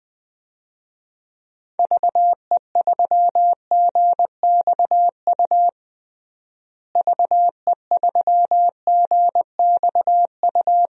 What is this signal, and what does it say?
Morse code, heard as a keyed tone, about 20 wpm, 700 Hz. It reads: VE3GXU VE3GXU
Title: DXCC-100 Call Signs - 6 Characters - Repeated 2x - 20wpm